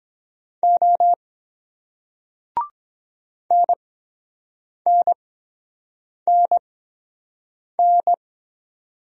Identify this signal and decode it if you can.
Morse code, heard as a keyed tone, about 23 wpm, spaced as for 7 wpm, 700 Hz. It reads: O NNNN